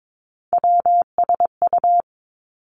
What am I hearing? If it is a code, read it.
Morse code, heard as a keyed tone, about 22 wpm, 700 Hz. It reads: WSU
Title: Three Challenging Letter-Number Combinations 22wpm